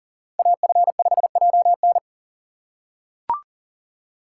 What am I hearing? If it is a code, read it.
Morse code, heard as a keyed tone, about 40 wpm, 700 Hz. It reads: AF5JN
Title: USA Call Signs - 5 Characters - 2x Character Spacing - 40wpm